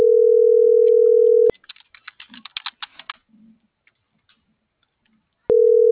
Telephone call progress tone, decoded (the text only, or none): ringback tone